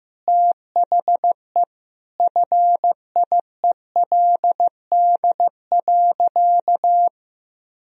Morse code, heard as a keyed tone, about 15 wpm, 700 Hz. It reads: THE FIELD.
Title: Sentences from Top 1000 Words - Review Part 2 - 15wpm